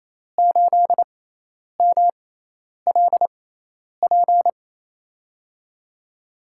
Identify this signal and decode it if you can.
Morse code, heard as a keyed tone, about 28 wpm, spaced as for 10 wpm, 700 Hz. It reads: 8MLP
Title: USA Call Signs - 5 Characters - 6x Character Spacing - 28wpm